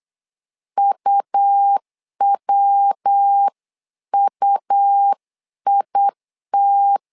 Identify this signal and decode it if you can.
Morse code, heard as a keyed tone, about 8 wpm, 800 Hz. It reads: UWUIT
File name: LEZIONI / 3 MWRO / 3 MW 8wpm 2501